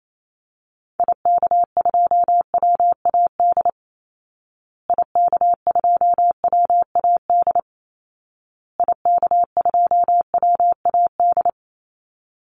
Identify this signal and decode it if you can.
Morse code, heard as a keyed tone, about 28 wpm, 700 Hz. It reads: IK2WAD IK2WAD IK2WAD
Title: Call Signs Repeated 2x - 28wpm